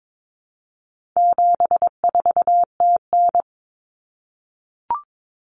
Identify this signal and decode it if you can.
Morse code, heard as a keyed tone, about 22 wpm, 700 Hz. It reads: 74TN
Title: Common Cut Numbers - 4-Digit 22wpm